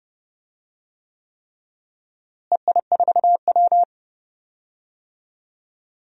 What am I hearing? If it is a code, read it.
Morse code, heard as a keyed tone, about 29 wpm, 700 Hz. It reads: EI4W
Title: DXCC-100 Call Signs - 4 Characters - Repeated 3x - 30wpm